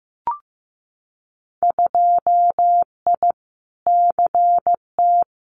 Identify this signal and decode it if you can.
Morse code, heard as a keyed tone, about 15 wpm, 700 Hz. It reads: 2I CT
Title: Winter Field Day Exchanges 15wpm